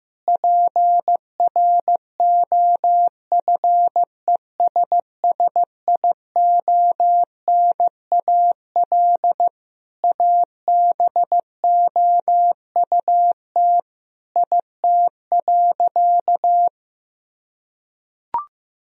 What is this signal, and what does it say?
Morse code, heard as a keyed tone, about 15 wpm, 700 Hz. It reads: PROFESSIONAL ABOUT IT.
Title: Sentences from Top 1000 Words - Review Part 4 - 15wpm